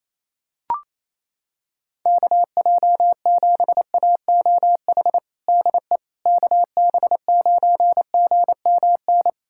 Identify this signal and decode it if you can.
Morse code, heard as a keyed tone, about 28 wpm, 700 Hz. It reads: KJ7AOH DE KB9GMN
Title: USA Callsign DE Callsign - 6 Characters - 28wpm